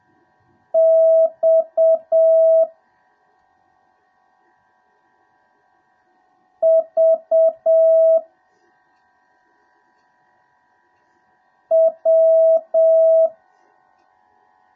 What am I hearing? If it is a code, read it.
Morse code, heard as a keyed tone, about 7 wpm, 650 Hz. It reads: X V W